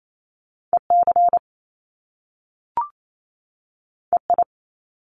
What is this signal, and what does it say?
Morse code, heard as a keyed tone, about 28 wpm, 700 Hz. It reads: E C E I
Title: Top Two Letter Combinations in Words 28wpm